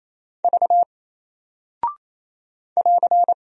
Morse code, heard as a keyed tone, about 28 wpm, 700 Hz. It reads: V <AR>
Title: Single Character (Letter:Number:Punctuation:Prosign) 28wpm